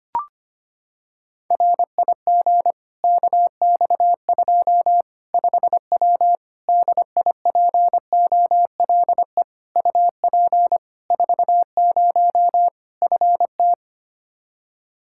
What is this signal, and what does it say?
Morse code, heard as a keyed tone, about 25 wpm, 700 Hz. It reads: RIG KX2 5W DIPOLE UP 40 FT
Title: Practice QSO - Exchange 2 - Info - 25wpm